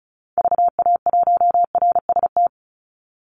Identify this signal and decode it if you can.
Morse code, heard as a keyed tone, about 34 wpm, 700 Hz. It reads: VA1RST
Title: Call Signs Repeated 3x - 35wpm